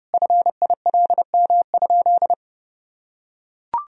Morse code, heard as a keyed tone, about 30 wpm, 700 Hz. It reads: FILM?
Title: Sentences from Top 1000 Words - Review Part 3 - 30wpm